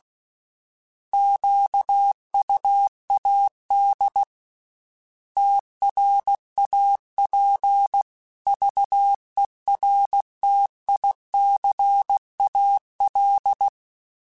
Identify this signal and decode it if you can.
Morse code, heard as a keyed tone, about 16 wpm, 800 Hz. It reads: QUAD TRAPVERTICAL